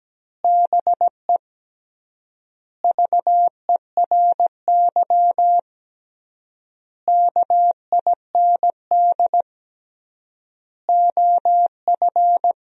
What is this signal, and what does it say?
Morse code, heard as a keyed tone, about 17 wpm, 700 Hz. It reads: BE VERY KIND OF
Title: 3x Word-Spacing - Sentences from Top 200 Words - 17wpm